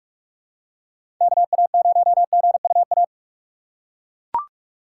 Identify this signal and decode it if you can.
Morse code, heard as a keyed tone, about 42 wpm, 700 Hz. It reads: KA0GUA
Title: USA Call Signs - 6 Characters - 2x Character Spacing - 45wpm